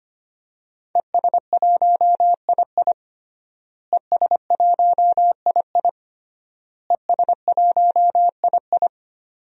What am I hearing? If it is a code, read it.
Morse code, heard as a keyed tone, about 25 wpm, 700 Hz. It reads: ES1II ES1II ES1II
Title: DXCC-100 Call Signs - 5 Characters - Repeated 3x - 25wpm